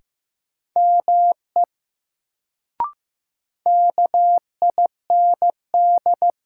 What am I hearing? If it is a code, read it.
Morse code, heard as a keyed tone, about 16 wpm, 700 Hz. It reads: ME KIND
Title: Sentences from Top 200 Words - New Word Drill - 15wpm